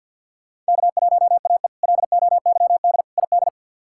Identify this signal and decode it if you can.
Morse code, heard as a keyed tone, about 53 wpm, 700 Hz. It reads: K1AE LOYD ID
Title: K1USN SST QSO Practice - 50wpm